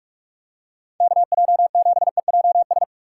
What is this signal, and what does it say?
Morse code, heard as a keyed tone, about 46 wpm, 700 Hz. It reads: KJ7EJS
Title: USA Call Signs - 6 Characters - 4x Character Spacing - 45wpm